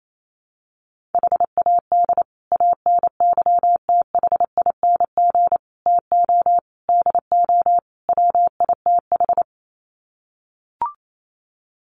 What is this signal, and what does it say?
Morse code, heard as a keyed tone, about 28 wpm, 700 Hz. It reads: HAD ANYTHING TO DO WITH
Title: Sets of 5 Words Encore 28wpm